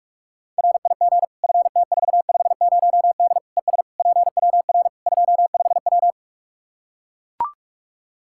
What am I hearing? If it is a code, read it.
Morse code, heard as a keyed tone, about 41 wpm, 700 Hz. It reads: RIG FT450D ES PWR 25W
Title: Practice QSO - Exchange 2 - Info - 45wpm